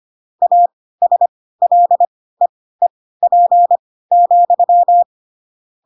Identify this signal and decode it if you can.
Morse code, heard as a keyed tone, about 25 wpm, spaced as for 16 wpm, 700 Hz. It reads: ASLEEP,